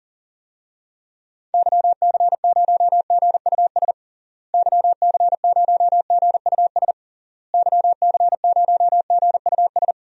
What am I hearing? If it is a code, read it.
Morse code, heard as a keyed tone, about 40 wpm, 700 Hz. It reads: YC0GUS YC0GUS YC0GUS
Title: DXCC-100 Call Signs - 6 Characters - Repeated 3x - 40wpm